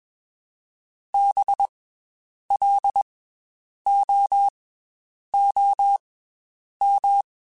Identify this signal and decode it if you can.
Morse code, heard as a keyed tone, about 21 wpm, spaced as for 8 wpm, 800 Hz. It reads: BLOOM
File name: Special INGs / Special INGs-mp3s / Slow ing @20